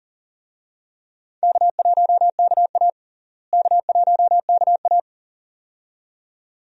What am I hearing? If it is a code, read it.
Morse code, heard as a keyed tone, about 40 wpm, 700 Hz. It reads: K1KA K1KA
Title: USA Call Signs - 4 Characters - Repeated 2x - 40wpm